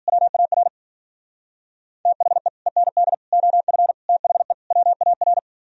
Morse code, heard as a keyed tone, about 50 wpm, 700 Hz. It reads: WAR THE END OF THE WAR
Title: Speed-Racing - Sets of 5 Words 40wpm